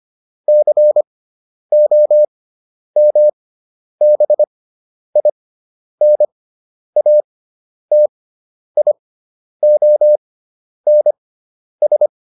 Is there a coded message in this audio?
Yes: Morse code from 0.5 to 12.1 s, heard as a keyed tone, about 25 wpm, spaced as for 10 wpm, 600 Hz. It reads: COMBINATIONS